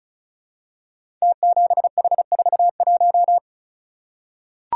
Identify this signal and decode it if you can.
Morse code, heard as a keyed tone, about 36 wpm, 700 Hz. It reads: T7H41
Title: Code Groups - 5 Letter:Numbers - 35wpm